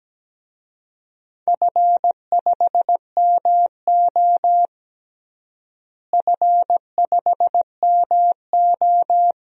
Morse code, heard as a keyed tone, about 17 wpm, 700 Hz. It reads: F5MO F5MO
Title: DXCC-100 Call Signs - 4 Characters - Repeated 2x - 17wpm